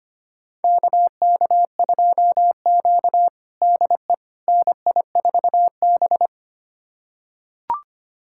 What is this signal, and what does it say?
Morse code, heard as a keyed tone, about 25 wpm, 700 Hz. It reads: KK2Q DE NI4B
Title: USA Callsign DE Callsign - 4 Characters - 25wpm